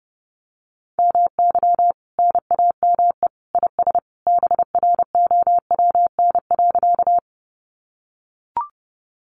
Morse code, heard as a keyed tone, about 30 wpm, 700 Hz. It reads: MY NAME IS BROWN.
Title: Sentences from Top 1200 Words 30wpm